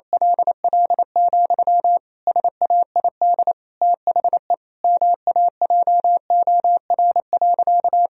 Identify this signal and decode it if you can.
Morse code, heard as a keyed tone, about 28 wpm, 700 Hz. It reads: LL, SAID THE MAJOR.
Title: Sentences from Top 1000 Words - Review Part 4 - 28wpm